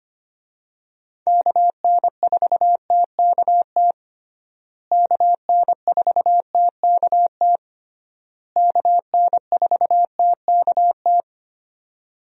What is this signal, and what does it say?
Morse code, heard as a keyed tone, about 25 wpm, 700 Hz. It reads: KN4TKT KN4TKT KN4TKT
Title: USA Call Signs - 6 Characters - Repeated 3x - 25wpm